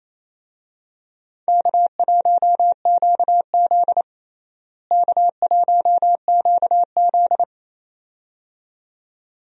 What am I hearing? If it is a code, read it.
Morse code, heard as a keyed tone, about 27 wpm, 700 Hz. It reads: K1QZ K1QZ
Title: USA Call Signs - 4 Characters - Repeated 2x - 28wpm